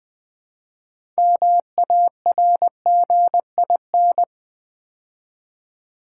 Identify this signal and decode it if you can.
Morse code, heard as a keyed tone, about 20 wpm, 700 Hz. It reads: MARGIN
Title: Build A Word - 2 Letters At A Time - 6-Letter Words - 20wpm